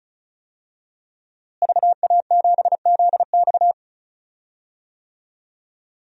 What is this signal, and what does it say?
Morse code, heard as a keyed tone, about 36 wpm, 700 Hz. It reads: VA7ZX
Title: DXCC-100 Call Signs - 5 Characters - Repeated 2x - 35wpm